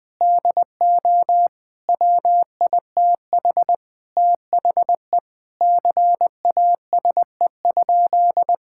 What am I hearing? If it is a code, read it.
Morse code, heard as a keyed tone, about 21 wpm, 700 Hz. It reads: DO WITH THE CASE?
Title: Sentences with 4-Word Phrases - Spoken Beginning - 20wpm